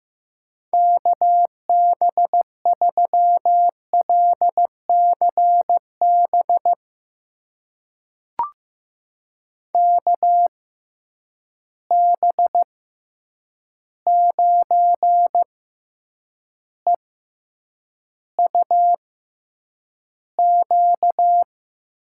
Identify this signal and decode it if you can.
Morse code, heard as a keyed tone, about 15 wpm, spaced as for 8 wpm, 700 Hz. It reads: KB3LCB KB9EUQ
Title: USA Call Signs - 6 Characters - 6x Character Spacing - 15wpm